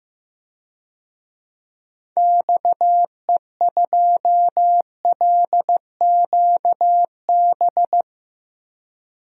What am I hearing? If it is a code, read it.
Morse code, heard as a keyed tone, about 15 wpm, 700 Hz. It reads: XE2LQB
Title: DXCC-100 Call Signs - 6 Characters - Repeated 3x - 15wpm